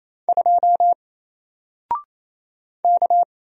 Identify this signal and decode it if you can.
Morse code, heard as a keyed tone, about 27 wpm, 700 Hz. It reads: 2 K
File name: Single-Letter-Number-28wpm